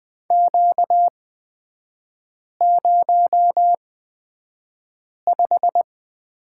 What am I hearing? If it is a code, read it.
Morse code, heard as a keyed tone, about 20 wpm, spaced as for 5 wpm, 700 Hz. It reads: Q05
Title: 243 Lesson 39e - Letter z - Three Characters 20wpm